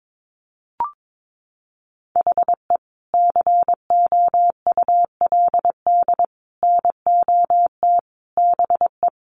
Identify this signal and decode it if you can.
Morse code, heard as a keyed tone, about 22 wpm, 700 Hz. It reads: HE COULD NOT BE
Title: Sentences from Top 1400 Words 22wpm